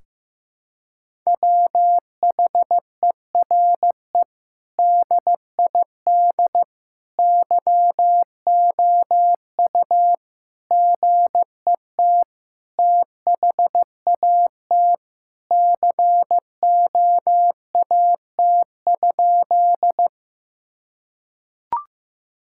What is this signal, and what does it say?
Morse code, heard as a keyed tone, about 15 wpm, 700 Hz. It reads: WHERE DID YOU GET THAT COAT?
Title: Sentences from Top 1600 Words 15wpm